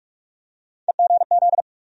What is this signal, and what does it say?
Morse code, heard as a keyed tone, about 41 wpm, 700 Hz. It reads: EGZ
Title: USA Call Sign Postfixes - 3 Letters 45wpm